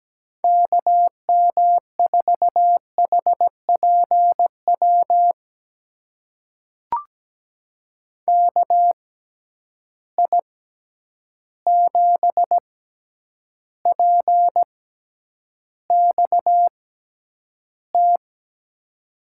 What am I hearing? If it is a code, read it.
Morse code, heard as a keyed tone, about 17 wpm, spaced as for 9 wpm, 700 Hz. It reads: KM4HPW KI7PXT
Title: USA Call Signs - 6 Characters - 6x Character Spacing - 17wpm